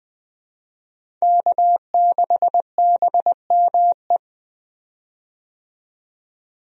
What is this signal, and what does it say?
Morse code, heard as a keyed tone, about 20 wpm, 700 Hz. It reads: K6BME
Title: USA Call Signs - 5 Characters - 20wpm